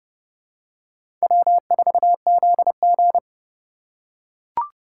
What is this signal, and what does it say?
Morse code, heard as a keyed tone, about 30 wpm, 700 Hz. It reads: W4ZG
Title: USA Call Signs - 4 Characters - 3x Character Spacing - 30wpm